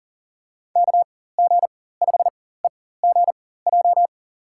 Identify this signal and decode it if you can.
Morse code, heard as a keyed tone, about 40 wpm, spaced as for 19 wpm, 700 Hz. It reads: KG5EGJ
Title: USA Call Signs - 6 Characters - 4x Character Spacing - 40wpm